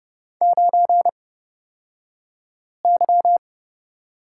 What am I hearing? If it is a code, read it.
Morse code, heard as a keyed tone, about 30 wpm, spaced as for 5 wpm, 700 Hz. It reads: 9Y
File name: Lesson 38d - Letter q - Two Characters 30wpm